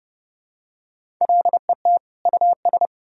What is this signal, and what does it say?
Morse code, heard as a keyed tone, about 30 wpm, 700 Hz. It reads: LET US
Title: Sentences from Top 1000 Words - Review Part 2 - 30wpm